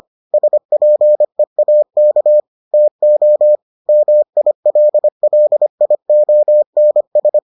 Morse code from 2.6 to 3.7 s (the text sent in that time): TO